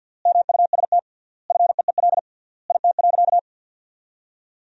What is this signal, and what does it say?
Morse code, heard as a keyed tone, about 50 wpm, 700 Hz. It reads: MUST FEEL IT.
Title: 3x Word-Spacing - Sentences from Top 400 Words - 50wpm